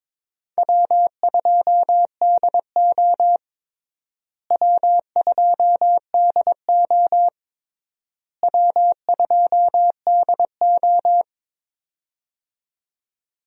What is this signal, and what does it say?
Morse code, heard as a keyed tone, about 22 wpm, 700 Hz. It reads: W2DO W2DO W2DO
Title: USA Call Signs - 4 Characters - Repeated 3x - 22wpm